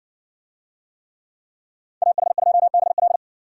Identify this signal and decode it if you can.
Morse code, heard as a keyed tone, about 54 wpm, 700 Hz. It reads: AH2BL